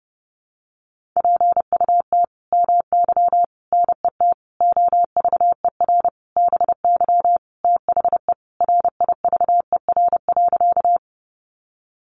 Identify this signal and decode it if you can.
Morse code, heard as a keyed tone, about 30 wpm, 700 Hz. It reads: PUT MY NET OVER BY THE RIVER.
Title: Sentences from Top 1400 Words 30wpm